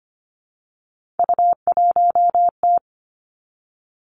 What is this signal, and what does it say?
Morse code, heard as a keyed tone, about 25 wpm, 700 Hz. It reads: U1T
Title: Three Challenging Letter-Number Combinations 25wpm